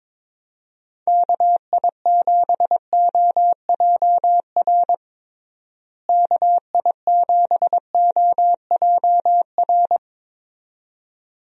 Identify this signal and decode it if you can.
Morse code, heard as a keyed tone, about 22 wpm, 700 Hz. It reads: KI7OJR KI7OJR
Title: USA Call Signs - 6 Characters - Repeated 2x - 22wpm